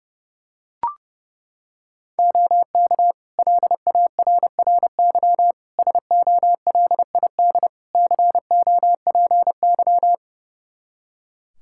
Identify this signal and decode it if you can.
Morse code, heard as a keyed tone, about 30 wpm, 700 Hz. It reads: OK LARRY SOLID COPY
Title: Practice QSO - Exchange 3 - More Info - 30wpm